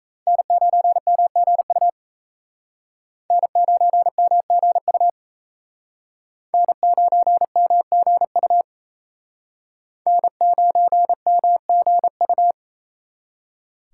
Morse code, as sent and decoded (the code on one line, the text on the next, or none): -. ----. -- --. ..- / -. ----. -- --. ..- / -. ----. -- --. ..- / -. ----. -- --. ..-
N9MGU N9MGU N9MGU N9MGU